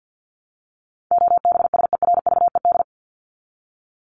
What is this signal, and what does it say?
Morse code, heard as a keyed tone, about 48 wpm, 700 Hz. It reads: OBSERVED
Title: ED Suffix - Top 100 Words with Base Words 50wpm